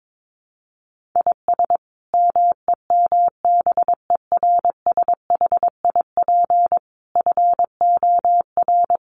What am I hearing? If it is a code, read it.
Morse code, heard as a keyed tone, about 22 wpm, 700 Hz. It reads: IS MEMBERSHIP FOR
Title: Sentences from Top 1400 Words 22wpm